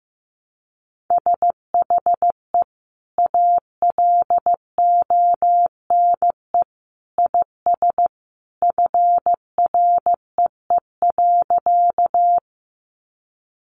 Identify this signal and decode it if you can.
Morse code, heard as a keyed tone, about 15 wpm, 700 Hz. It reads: SHE ALONE IS FREE.
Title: Sentences from Top 1000 Words 15wpm